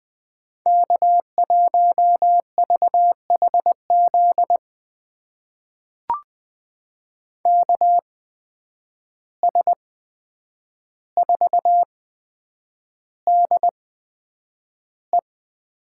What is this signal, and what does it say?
Morse code, heard as a keyed tone, about 20 wpm, spaced as for 9 wpm, 700 Hz. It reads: K1VHZ KS4DE
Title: USA Call Signs - 5 Characters - 8x Character Spacing - 20wpm